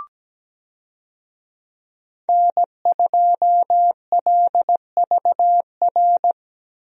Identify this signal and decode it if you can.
Morse code, heard as a keyed tone, about 17 wpm, 700 Hz. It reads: N2LVR
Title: USA Call Signs - 5 Characters - Repeated 3x - 17wpm